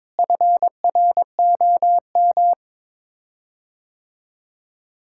Morse code, heard as a keyed tone, about 22 wpm, 700 Hz. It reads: FROM
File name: Sets of 9 Words - 22wpm